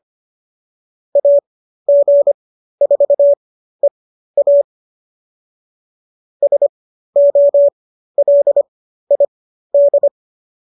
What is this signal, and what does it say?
Morse code, heard as a keyed tone, about 25 wpm, spaced as for 13 wpm, 570 Hz. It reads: AG4EA SOLID